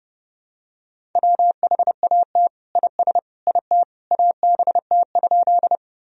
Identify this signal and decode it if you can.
Morse code, heard as a keyed tone, about 30 wpm, 700 Hz. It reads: WHAT IS IT ABT?
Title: Sentences from Top 100 Words and Abbreviations 30wpm